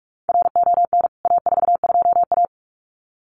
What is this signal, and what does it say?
Morse code, heard as a keyed tone, about 44 wpm, 700 Hz. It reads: RON A42A
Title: CWT Exchange 45wpm